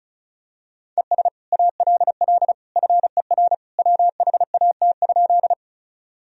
Morse code, heard as a keyed tone, about 36 wpm, 700 Hz. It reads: ES ALL FER WHAT?
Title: Sentences from Top 100 Words and Abbreviations 35wpm